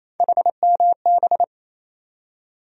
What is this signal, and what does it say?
Morse code, heard as a keyed tone, about 28 wpm, 700 Hz. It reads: HMB
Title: USA Call Sign Postfixes - 3 Letters 28wpm